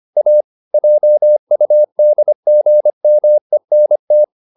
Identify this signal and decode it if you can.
Morse code, heard as a keyed tone, about 25 wpm, 600 Hz. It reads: A JUDGMENT